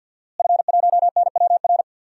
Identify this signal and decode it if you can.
Morse code, heard as a keyed tone, about 47 wpm, 700 Hz. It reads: F1NWR
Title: DXCC-100 Call Signs - 5 Characters - Repeated 2x - 50wpm